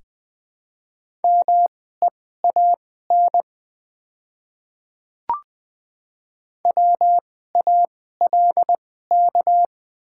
Morse code, heard as a keyed tone, about 20 wpm, spaced as for 15 wpm, 700 Hz. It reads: MEAN WALK
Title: 2x Character Spacing - Top 300 Words - 20wpm